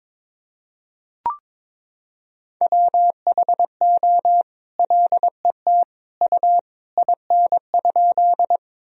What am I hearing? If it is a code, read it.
Morse code, heard as a keyed tone, about 22 wpm, 700 Hz. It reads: WHO LET U IN?
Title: Sentences from Top 300 Words and Abbreviations 22wpm